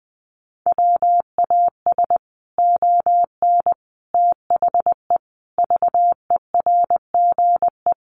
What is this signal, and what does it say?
Morse code, heard as a keyed tone, about 20 wpm, 700 Hz. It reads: WAS ON THE VERGE
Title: Sets of 5 Words Encore 20wpm